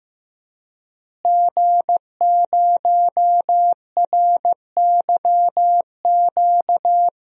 Morse code, heard as a keyed tone, about 16 wpm, 700 Hz. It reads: G0RYQ
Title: DXCC-100 Call Signs - 5 Characters - Repeated 3x - 15wpm